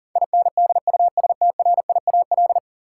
Morse code, heard as a keyed tone, about 40 wpm, 700 Hz. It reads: INDUSTRIAL
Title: Sets of 5 Words - New Word Drill - 40wpm